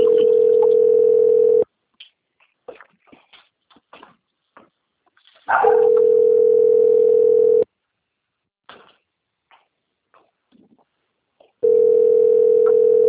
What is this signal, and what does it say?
Ringback tone (North America), 440 + 480 Hz.
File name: rg-1600-unknown- - - .3503